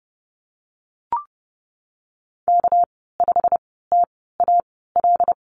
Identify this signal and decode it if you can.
Morse code, heard as a keyed tone, about 30 wpm, spaced as for 17 wpm, 700 Hz. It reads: K5TAL
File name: Lesson 37g - Letter x - Call Signs 30wpm